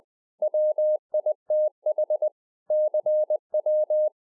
Morse code, heard as a keyed tone, about 20 wpm, 610 Hz. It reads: WITH CW
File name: 2011-04-mp3 / A SMALL TRANCEIVER 3 20WPM